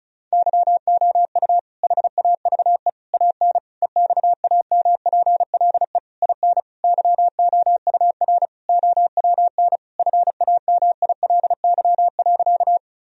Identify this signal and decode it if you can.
Morse code, heard as a keyed tone, about 36 wpm, 700 Hz. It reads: YOU HAVE AN EXAMPLE IN YOUR OWN FAMILY.
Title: Sentences from Top 1000 Words - Review Part 1 - 35wpm